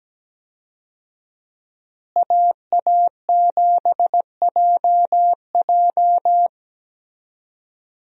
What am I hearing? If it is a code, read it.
Morse code, heard as a keyed tone, about 17 wpm, 700 Hz. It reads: AA7JJ
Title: USA Call Signs - 5 Characters - Repeated 3x - 17wpm